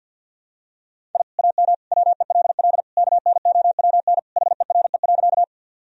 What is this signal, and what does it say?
Morse code, heard as a keyed tone, about 48 wpm, 700 Hz. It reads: I AM WELL KNOWN HERE.
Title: Sentences from Top 1000 Words - Review Part 3 - 50wpm